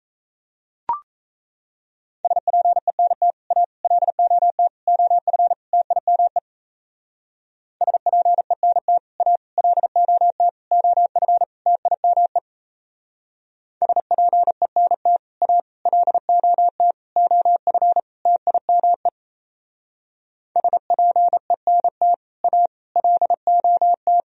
Morse code, heard as a keyed tone, about 33 wpm, 700 Hz. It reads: SPENT A LOT OF TIME SPENT A LOT OF TIME SPENT A LOT OF TIME SPENT A LOT
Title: Speed-Racing - Sets of 5 Words 28wpm